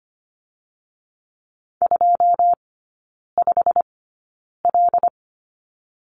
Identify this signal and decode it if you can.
Morse code, heard as a keyed tone, about 25 wpm, spaced as for 9 wpm, 700 Hz. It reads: 25L
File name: Lesson 17e - Letter u - Three Characters 25wpm